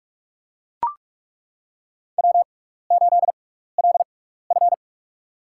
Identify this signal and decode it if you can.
Morse code, heard as a keyed tone, about 41 wpm, spaced as for 15 wpm, 700 Hz. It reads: W8LF
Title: USA Call Signs - 6x Character Spacing - 45wpm